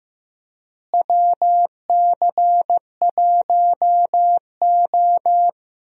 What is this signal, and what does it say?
Morse code, heard as a keyed tone, about 15 wpm, 700 Hz. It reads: WC1O
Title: USA Call Signs - 4 Characters - Repeated 2x - 15wpm